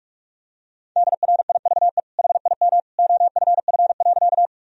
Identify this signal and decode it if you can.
Morse code, heard as a keyed tone, about 42 wpm, 700 Hz. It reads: DRIVE HIM OFF.
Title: Sentences from Top 1000 Words - Review Part 2 - 45wpm